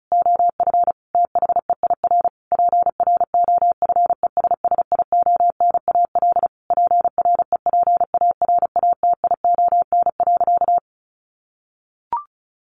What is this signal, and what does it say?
Morse code, heard as a keyed tone, about 34 wpm, 700 Hz. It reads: OF THEIR PROFESSIONAL PREPARATION.
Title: Sentences from Top 1400 Words 35wpm